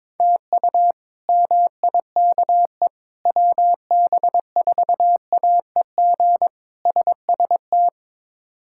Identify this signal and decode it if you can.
Morse code, heard as a keyed tone, about 22 wpm, 700 Hz. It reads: TU MIKE WB4AEG SST
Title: K1USN SST QSO Practice - 22wpm